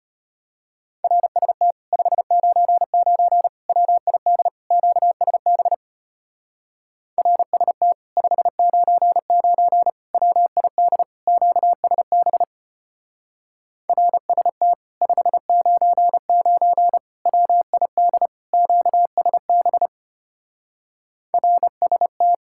Morse code, heard as a keyed tone, about 33 wpm, 700 Hz. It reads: RST 599 WID QSB RST 599 WID QSB RST 599 WID QSB RST